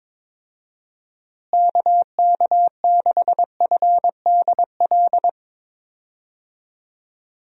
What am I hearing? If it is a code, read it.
Morse code, heard as a keyed tone, about 21 wpm, 700 Hz. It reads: KK6FDL
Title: USA Call Signs - 6 Characters - Repeated 3x - 22wpm